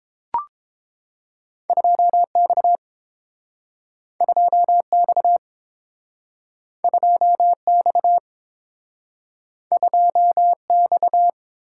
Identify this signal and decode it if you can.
Morse code, heard as a keyed tone, about 32 wpm, 700 Hz. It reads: 2X 2X 2X 2X